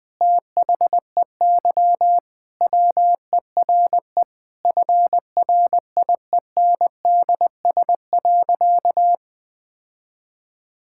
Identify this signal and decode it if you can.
Morse code, heard as a keyed tone, about 20 wpm, 700 Hz. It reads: THEY WERE FRIENDS.
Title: Sentences with 3-Word Phrases - Spoken Beginning - 20wpm